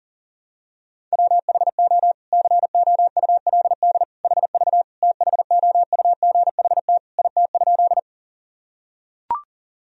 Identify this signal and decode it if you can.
Morse code, heard as a keyed tone, about 40 wpm, 700 Hz. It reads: WHO COULD HV THOUGHT IT?
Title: Sentences from Top 300 Words and Abbreviations 40wpm